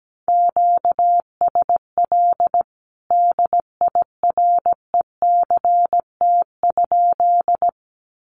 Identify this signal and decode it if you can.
Morse code, heard as a keyed tone, about 17 wpm, 700 Hz. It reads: QSL DIRECT?
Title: Common QSO Phrases 17wpm